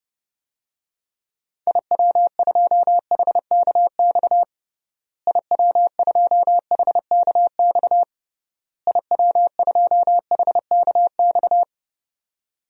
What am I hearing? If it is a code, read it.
Morse code, heard as a keyed tone, about 30 wpm, 700 Hz. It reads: IW2HKX IW2HKX IW2HKX
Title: DXCC-100 Call Signs - 6 Characters - Repeated 3x - 30wpm